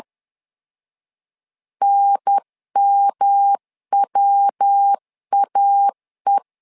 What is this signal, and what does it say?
Morse code, heard as a keyed tone, about 10 wpm, 800 Hz. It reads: NMWAE
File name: LEZIONI / 3 MWRO / 3 MWR 10wpm 1952